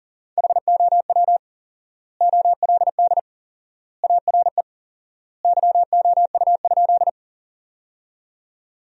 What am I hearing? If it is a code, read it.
Morse code, heard as a keyed tone, about 40 wpm, 700 Hz. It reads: HOW OLD ARE YOU?
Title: 4x Word-Spacing - Sentences from Top 200 Words - 40wpm